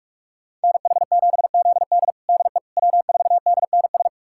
Morse code, heard as a keyed tone, about 46 wpm, 700 Hz. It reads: NH7ZD DE W4DNS
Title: USA Callsign DE Callsign - 5 Characters - 45wpm